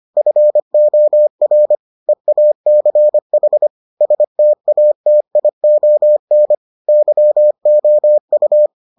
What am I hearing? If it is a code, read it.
Morse code, heard as a keyed tone, about 25 wpm, 600 Hz. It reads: FOR EACH STATION YOU